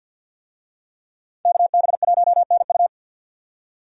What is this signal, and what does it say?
Morse code, heard as a keyed tone, about 51 wpm, 700 Hz. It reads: KB1NU
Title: USA Call Signs - 5 Characters - 50wpm